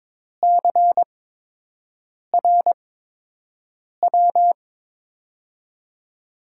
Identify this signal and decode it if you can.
Morse code, heard as a keyed tone, about 22 wpm, spaced as for 6 wpm, 700 Hz. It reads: CRW
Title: USA Call Signs - 6 Characters - 8x Character Spacing - 22wpm